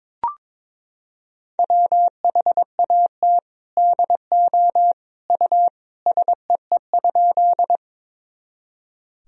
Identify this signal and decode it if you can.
Morse code, heard as a keyed tone, about 22 wpm, 700 Hz. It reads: WHAT DO U SEE?
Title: Sentences from Top 100 Words and Abbreviations 22wpm